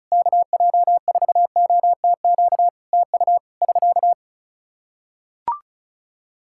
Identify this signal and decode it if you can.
Morse code, heard as a keyed tone, about 34 wpm, 700 Hz. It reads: KJ4OTQ TU <SK>
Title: Practice QSO - Exchange 4 - Ending - 35wpm